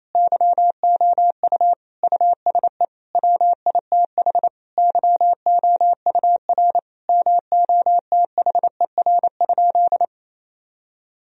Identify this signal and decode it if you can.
Morse code, heard as a keyed tone, about 28 wpm, 700 Hz. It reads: YOU USE WITH YOUR MOTHER?
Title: Sentences from Top 1000 Words - Review Part 3 - 28wpm